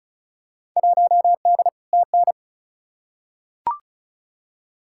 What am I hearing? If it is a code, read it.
Morse code, heard as a keyed tone, about 36 wpm, 700 Hz. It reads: 1D TN
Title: Field Day Exchanges - 35wpm